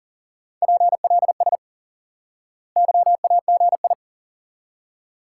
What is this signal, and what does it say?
Morse code, heard as a keyed tone, about 37 wpm, 700 Hz. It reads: PLS YAGI
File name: IQE Course 3 - Lesson 89 - YAGI - Two Elements - 40wpm